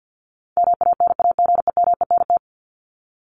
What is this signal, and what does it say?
Morse code, heard as a keyed tone, about 50 wpm, 700 Hz. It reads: MANAGEMENT
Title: Top 501-750 Words 50wpm